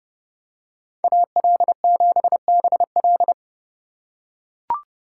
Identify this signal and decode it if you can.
Morse code, heard as a keyed tone, about 30 wpm, 700 Hz. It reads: AL7BL
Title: USA Call Signs - 5 Characters - 2x Character Spacing - 30wpm